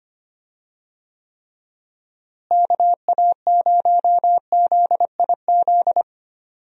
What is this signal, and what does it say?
Morse code, heard as a keyed tone, about 25 wpm, 700 Hz. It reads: KA0ZIZ
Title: USA Call Signs - 6 Characters - Repeated 3x - 25wpm